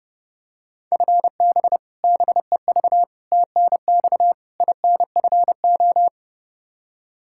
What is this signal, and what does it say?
Morse code, heard as a keyed tone, about 30 wpm, 700 Hz. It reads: FB BEV TNX INFO
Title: Practice QSO - Exchange 3 - More Info - 30wpm